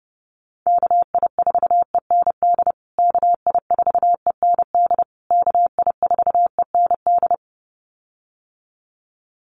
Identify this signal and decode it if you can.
Morse code, heard as a keyed tone, about 30 wpm, 700 Hz. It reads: KI4END KI4END KI4END
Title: USA Call Signs Repeated 3x 30wpm